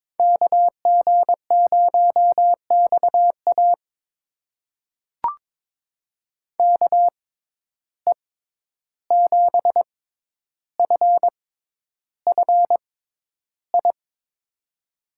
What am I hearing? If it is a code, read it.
Morse code, heard as a keyed tone, about 22 wpm, spaced as for 11 wpm, 700 Hz. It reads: KG0XA KE7FFI
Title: USA Call Signs - 6x Character Spacing - 22wpm